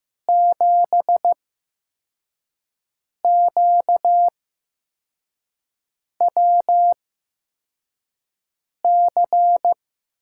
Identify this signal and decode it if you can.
Morse code, heard as a keyed tone, about 15 wpm, spaced as for 4 wpm, 700 Hz. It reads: 7QWC
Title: USA Call Signs - 6 Characters - 8x Character Spacing - 15wpm